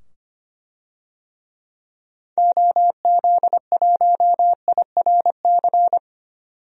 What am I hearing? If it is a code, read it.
Morse code, heard as a keyed tone, about 25 wpm, 700 Hz. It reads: OZ1IRC
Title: DXCC-100 Call Signs - 6 Characters - 25wpm